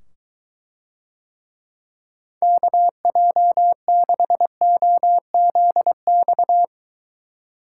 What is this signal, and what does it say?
Morse code, heard as a keyed tone, about 23 wpm, 700 Hz. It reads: KJ6OZX